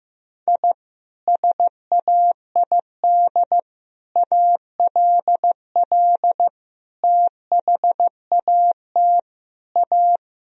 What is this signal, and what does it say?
Morse code, heard as a keyed tone, about 16 wpm, 700 Hz. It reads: I SAID ALL THAT A
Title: Sentences from Top 1000 Words - Review Part 2 - 15wpm